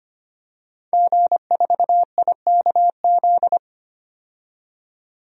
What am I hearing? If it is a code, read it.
Morse code, heard as a keyed tone, about 25 wpm, 700 Hz. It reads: G4IKZ
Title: DXCC-100 Call Signs - 5 Characters - 25wpm